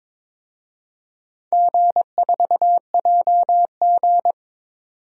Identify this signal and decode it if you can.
Morse code, heard as a keyed tone, about 21 wpm, 700 Hz. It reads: G4JG
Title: DXCC-100 Call Signs - 4 Characters - 22wpm